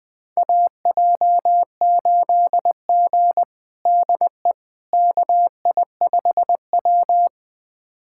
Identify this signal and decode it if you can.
Morse code, heard as a keyed tone, about 20 wpm, 700 Hz. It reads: AJ8G DE KI5W
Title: USA Callsign DE Callsign - 4 Characters - 20wpm